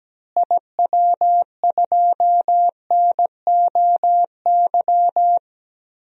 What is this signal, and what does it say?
Morse code, heard as a keyed tone, about 17 wpm, 700 Hz. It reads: IW2NOY
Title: DXCC-100 Call Signs - 6 Characters - Repeated 2x - 17wpm